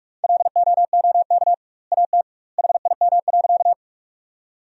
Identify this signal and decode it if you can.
Morse code, heard as a keyed tone, about 46 wpm, 700 Hz. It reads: LOOK AT HIM.
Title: 2x Word-Spacing - Sentences from Top 100 Words - 45wpm